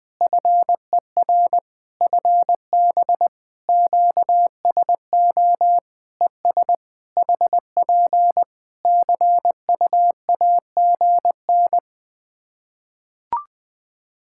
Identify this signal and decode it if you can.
Morse code, heard as a keyed tone, about 21 wpm, 700 Hz. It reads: FER FB QSO ES HP CUAGN
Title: K1USN SST QSO Practice 20wpm